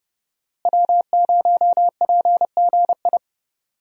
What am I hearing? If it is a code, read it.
Morse code, heard as a keyed tone, about 30 wpm, 700 Hz. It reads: W0PGI
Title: USA Call Signs - 5 Characters - Repeated 3x - 30wpm